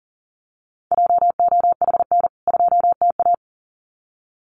Morse code, heard as a keyed tone, about 40 wpm, 700 Hz. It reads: JOHN 2TA
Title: CWT Exchange 40wpm